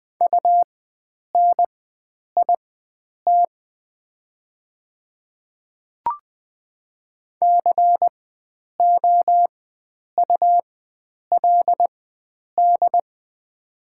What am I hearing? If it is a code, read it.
Morse code, heard as a keyed tone, about 20 wpm, spaced as for 9 wpm, 700 Hz. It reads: UNIT COULD
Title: 4x Character Spacing - Top 500 Words - 20wpm